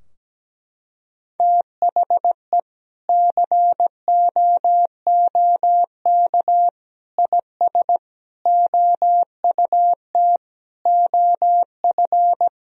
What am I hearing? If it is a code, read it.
Morse code, heard as a keyed tone, about 17 wpm, 700 Hz. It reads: THE COOK IS OUT OF
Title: Sentences from Top 1000 Words - Review Part 4 - 17wpm